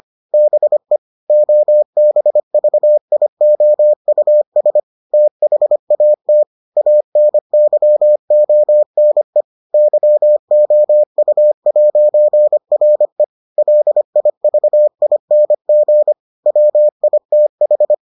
Morse code, heard as a keyed tone, about 25 wpm, 600 Hz. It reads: BE OBVIOUS THAT ANYONE YOU'RE LIVING WITH